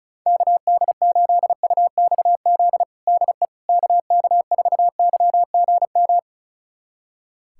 Morse code, heard as a keyed tone, about 34 wpm, 700 Hz. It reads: KD8UXZ DE KK4YGM
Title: USA Callsign DE Callsign - 6 Characters - 35wpm